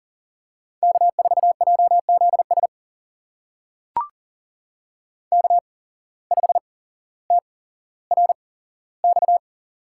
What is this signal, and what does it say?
Morse code, heard as a keyed tone, about 40 wpm, spaced as for 17 wpm, 700 Hz. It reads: K4JZS K5TRX
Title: USA Call Signs - 5 Characters - 8x Character Spacing - 40wpm